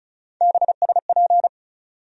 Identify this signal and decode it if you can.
Morse code, heard as a keyed tone, about 36 wpm, 700 Hz. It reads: BSP
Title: USA Call Sign Postfixes - 3 Letters 35wpm